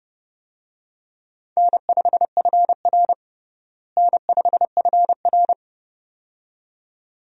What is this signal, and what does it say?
Morse code, heard as a keyed tone, about 32 wpm, 700 Hz. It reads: N5FR N5FR
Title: USA Call Signs - 4 Characters - Repeated 2x - 30wpm